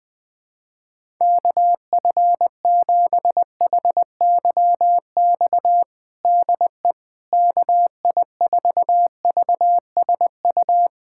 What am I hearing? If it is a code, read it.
Morse code, heard as a keyed tone, about 21 wpm, 700 Hz. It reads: KF7HYX DE KI4VSU
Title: USA Callsign DE Callsign - 6 Characters - 20wpm